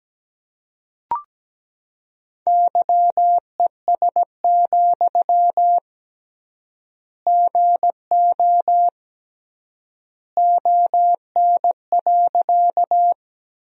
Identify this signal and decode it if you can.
Morse code, heard as a keyed tone, about 17 wpm, 700 Hz. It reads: YES, GO ON.
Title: 3x Word-Spacing - Sentences from Top 500 Words - 17wpm